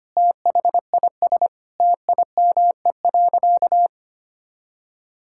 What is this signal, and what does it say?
Morse code, heard as a keyed tone, about 25 wpm, 700 Hz. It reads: THIS TIME.
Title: Sentences from Top 1000 Words - Review Part 5 - 25wpm